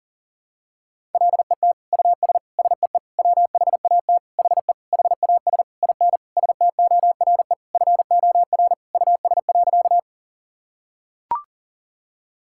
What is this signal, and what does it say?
Morse code, heard as a keyed tone, about 40 wpm, 700 Hz. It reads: LET US SEE WHAT HE HAS IN STORE FOR US.
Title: Sentences from Top 1000 Words - Review Part 5 - 40wpm